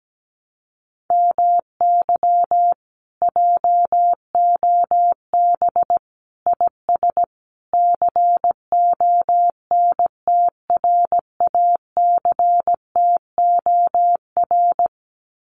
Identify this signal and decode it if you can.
Morse code, heard as a keyed tone, about 17 wpm, 700 Hz. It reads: MY JOB IS CONTRACTOR
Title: Common QSO Phrases Encore 17wpm